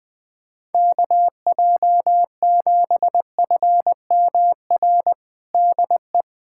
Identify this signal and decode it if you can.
Morse code, heard as a keyed tone, about 21 wpm, 700 Hz. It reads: KJ7FMR DE
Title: USA Callsign DE Callsign 20wpm